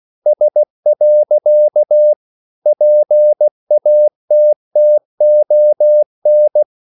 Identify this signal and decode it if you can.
Morse code, heard as a keyed tone, about 16 wpm, 600 Hz. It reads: S. PATTON
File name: CW / Practice / EnglishQuotations0116